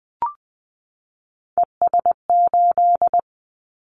Morse code, heard as a keyed tone, about 20 wpm, 700 Hz. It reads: ES8
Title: Callsign Prefixes for Top 100 DXCC - 20wpm